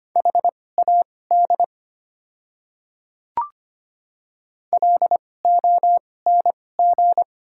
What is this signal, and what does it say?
Morse code, heard as a keyed tone, about 25 wpm, spaced as for 18 wpm, 690 Hz. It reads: HAD LONG
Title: 2x Character Spacing - Top 100 Words - 25wpm